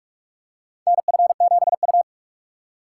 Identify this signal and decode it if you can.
Morse code, heard as a keyed tone, about 41 wpm, 700 Hz. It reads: NF7U
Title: USA Call Signs - 2x Character Spacing - 45wpm